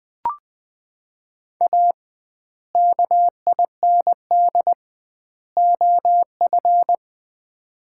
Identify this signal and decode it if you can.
Morse code, heard as a keyed tone, about 20 wpm, 700 Hz. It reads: A KIND OF
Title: 2x Word-Spacing - Sets of 3 Words - 20wpm